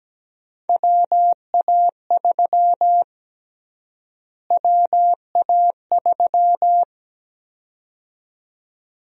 Morse code, heard as a keyed tone, about 17 wpm, 700 Hz. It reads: WA3 WA3
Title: USA Call Sign Prefixes - Repeated 2x - 17wpm